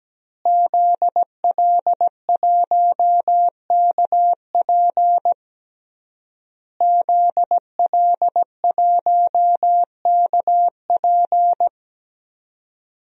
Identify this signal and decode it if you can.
Morse code, heard as a keyed tone, about 17 wpm, 700 Hz. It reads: ZL1KP ZL1KP
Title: DXCC-100 Call Signs - 5 Characters - Repeated 3x - 17wpm